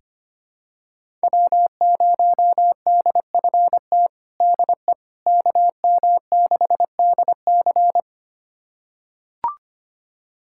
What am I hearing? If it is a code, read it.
Morse code, heard as a keyed tone, about 25 wpm, 700 Hz. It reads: W0DFT DE KM6DC
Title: USA Callsign DE Callsign - 5 Characters - 25wpm